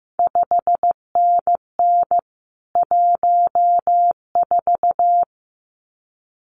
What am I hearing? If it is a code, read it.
Morse code, heard as a keyed tone, about 15 wpm, 700 Hz. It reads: 5NN 14
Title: CQ WW DX Contest Exchange - 15wpm